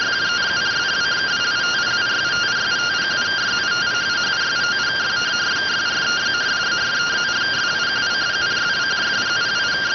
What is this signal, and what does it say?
Two-tone teleprinter signal, about 170 Hz shift.